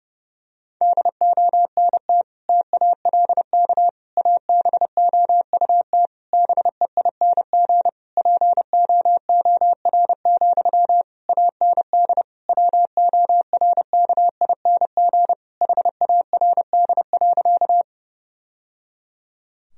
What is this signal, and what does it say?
Morse code, heard as a keyed tone, about 30 wpm, 700 Hz. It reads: DONT TALK ABOUT BEING POOR, AND WORKING HARD.
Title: Sentences from Top 1000 Words - Review Part 4 - 30wpm